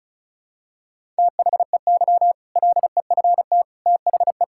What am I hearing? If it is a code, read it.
Morse code, heard as a keyed tone, about 36 wpm, 700 Hz. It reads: THEY LEFT THE
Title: Sentences from Top 1000 Words - Review Part 6 - 35wpm